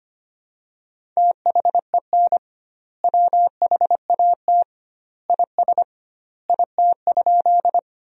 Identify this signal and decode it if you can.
Morse code, heard as a keyed tone, about 25 wpm, 700 Hz. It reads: THEN WHAT IS IT?
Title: 2x Word-Spacing - Sentences from Top 100 Words - 25wpm